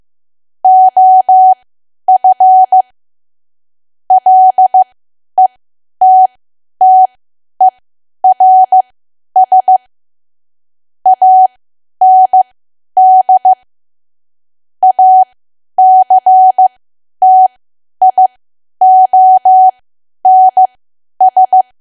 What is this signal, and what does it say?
Morse code, heard as a keyed tone, about 15 wpm, spaced as for 10 wpm, 750 Hz. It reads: OF LETTERS AND ACTIONS
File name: cwa / W1AW-Level2 / 10WPM